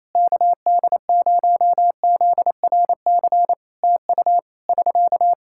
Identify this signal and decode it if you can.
Morse code, heard as a keyed tone, about 27 wpm, 700 Hz. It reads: KD0ZRC TU <SK>
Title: Practice QSO - Exchange 4 - Ending - 28wpm